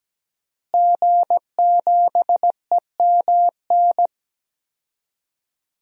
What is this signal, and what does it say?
Morse code, heard as a keyed tone, about 17 wpm, 700 Hz. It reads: G7EMN
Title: DXCC-100 Call Signs - 5 Characters - Repeated 3x - 17wpm